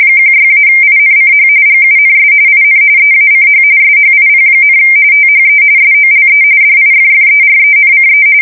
RTTY, about 45.45 baud. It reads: 16-INCH GUNS IN EACH SHIP, THE SOLE DIFFERENCE
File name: fsk170 / battleships